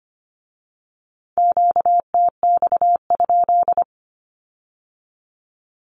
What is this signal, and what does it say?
Morse code, heard as a keyed tone, about 25 wpm, 700 Hz. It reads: QTX?
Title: Q-Codes with Spoken Meaning - 25wpm